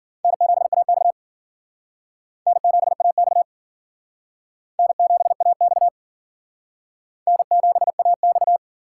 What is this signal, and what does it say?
Morse code, heard as a keyed tone, about 52 wpm, 700 Hz. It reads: N7AX N7AX N7AX N7AX